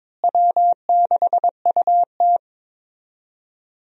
Morse code, heard as a keyed tone, about 22 wpm, 700 Hz. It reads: W6UT
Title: USA Call Signs - 4 Characters - Repeated 2x - 22wpm